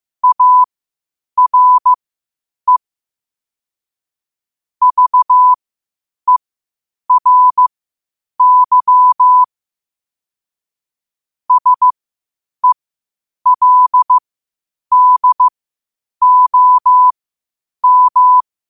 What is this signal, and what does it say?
Morse code, heard as a keyed tone, about 15 wpm, spaced as for 8 wpm, 1000 Hz. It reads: ARE VERY SELDOM
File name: qotdfiles / 8wpm